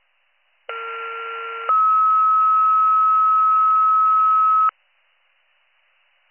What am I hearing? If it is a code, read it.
Two-tone page, heard as a one-second tone, then a three-second tone, about 1370 then 1290 Hz. It is heard from 0.7 to 4.7 s.